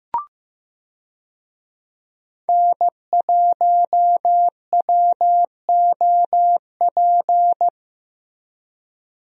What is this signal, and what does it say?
Morse code, heard as a keyed tone, about 15 wpm, 700 Hz. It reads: N1WOP
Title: USA Call Signs - 5 Characters - Repeated 3x - 15wpm